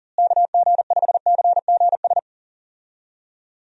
Morse code, heard as a keyed tone, about 40 wpm, 700 Hz. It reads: KG5CGS
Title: USA Call Signs - 6 Characters - Repeated 3x - 40wpm